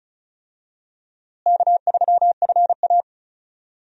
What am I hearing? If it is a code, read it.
Morse code, heard as a keyed tone, about 36 wpm, 700 Hz. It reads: K3FA
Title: USA Call Signs - 4 Characters - Repeated 3x - 35wpm